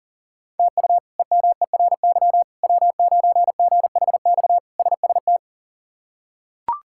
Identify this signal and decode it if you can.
Morse code, heard as a keyed tone, about 40 wpm, 700 Hz. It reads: TU EMERY W9GHX SST
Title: K1USN SST QSO Practice - 40wpm